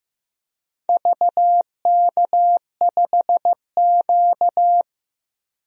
Morse code, heard as a keyed tone, about 15 wpm, 700 Hz. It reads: VK5Q
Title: DXCC-100 Call Signs - 4 Characters - Repeated 2x - 15wpm